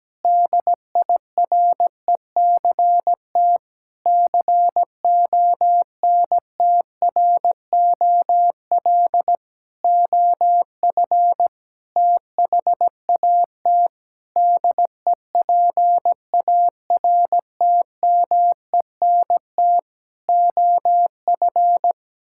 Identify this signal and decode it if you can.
Morse code, heard as a keyed tone, about 17 wpm, 700 Hz. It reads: DIRECT CONTROL OF THAT DEPARTMENT OF
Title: Sentences from Top 1000 Words - Review Part 5 - 17wpm